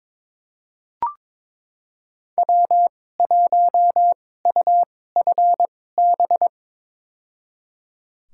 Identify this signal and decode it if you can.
Morse code, heard as a keyed tone, about 22 wpm, spaced as for 16 wpm, 700 Hz. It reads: W1UFB
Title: USA Call Signs - 5 Characters - 2x Character Spacing - 22wpm